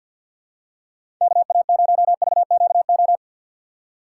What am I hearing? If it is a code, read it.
Morse code, heard as a keyed tone, about 48 wpm, 700 Hz. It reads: KA0VQO
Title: USA Call Signs - 6 Characters - 6x Character Spacing - 50wpm